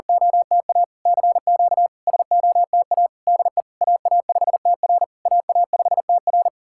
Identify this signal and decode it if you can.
Morse code, heard as a keyed tone, about 44 wpm, 700 Hz. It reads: OTA CQ SOTA DE AA5TR AA5TR
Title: SOTA QSO Practice 40wpm